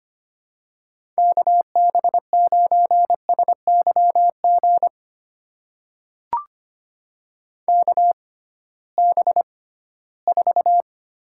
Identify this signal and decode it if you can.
Morse code, heard as a keyed tone, about 25 wpm, spaced as for 16 wpm, 700 Hz. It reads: KB9SYG KB4
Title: USA Call Signs - 6 Characters - 6x Character Spacing - 25wpm